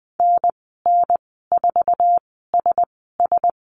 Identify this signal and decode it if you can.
Morse code, heard as a keyed tone, about 20 wpm, spaced as for 14 wpm, 700 Hz. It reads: NN4SS
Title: 048 Lesson 9g - Number 4 - Call Signs 20wpm